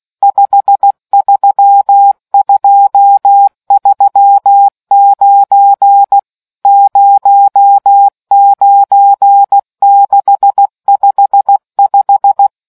Morse code, heard as a keyed tone, about 16 wpm, 800 Hz. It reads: 53239 09655